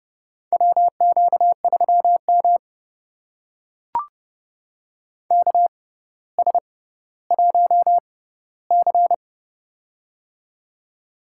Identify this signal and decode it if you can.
Morse code, heard as a keyed tone, about 30 wpm, spaced as for 15 wpm, 700 Hz. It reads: WQ3M KS1C
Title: USA Call Signs - 4 Characters - 6x Character Spacing - 30wpm